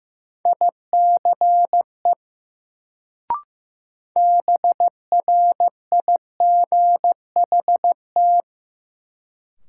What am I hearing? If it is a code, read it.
Morse code, heard as a keyed tone, about 15 wpm, 700 Hz. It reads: ICE BRIGHT
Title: Sentences from Top 1000 Words - New Word Drill - 15wpm